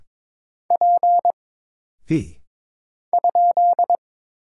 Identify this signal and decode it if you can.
Morse code, heard as a keyed tone, about 22 wpm, 700 Hz. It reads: P ?
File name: Single Character (Letter:Number:Punctuation:Prosign) - Mind-Melt - 22wpm